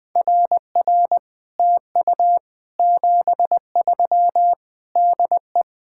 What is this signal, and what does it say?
Morse code, heard as a keyed tone, about 20 wpm, 700 Hz. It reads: RR TU 73 DE
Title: Speed-Racing Common QSO Phrases - 20wpm